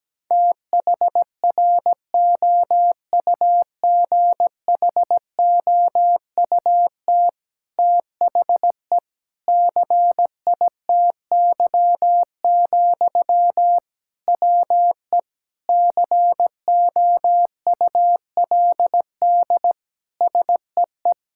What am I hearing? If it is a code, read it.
Morse code, heard as a keyed tone, about 17 wpm, 700 Hz. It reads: THROUGHOUT THE CITY, WE COULD SEE
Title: Sentences from Top 1000 Words - Review Part 5 - 17wpm